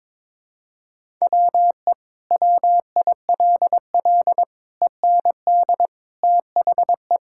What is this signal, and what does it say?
Morse code, heard as a keyed tone, about 21 wpm, 700 Hz. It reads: WE WILL END THE
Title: Sentences from Top 1000 Words - Review Part 6 - 22wpm